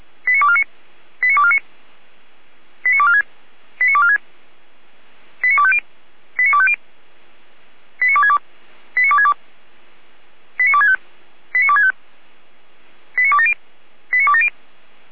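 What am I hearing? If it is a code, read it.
Five-tone selective call (ZVEI), heard as quick runs of five tones, tones of about 70 ms. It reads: 89249 89249 89246 89246 89250 89250 89262 89262 89265 89265 89270 89270
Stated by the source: ZVEI five-tone selective call 89115 received on SDR.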